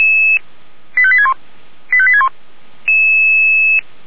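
Five-tone selective call (ZVEI), heard as quick runs of five tones, tones of about 70 ms. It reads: 85861 85861
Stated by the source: ZVEI five-tone selective call 85861 received via SDR.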